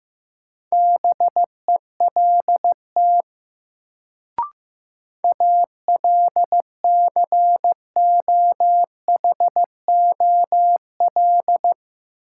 Morse code, heard as a keyed tone, about 15 wpm, 700 Hz. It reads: BELT ALCOHOL
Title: Sentences from Top 1400 Words - New Word Drill Alt - 15wpm